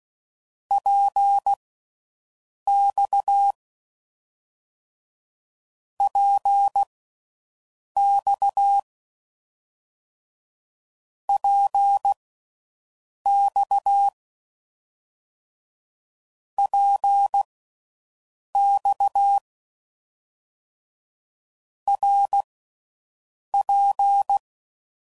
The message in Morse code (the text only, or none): PX PX PX PX RP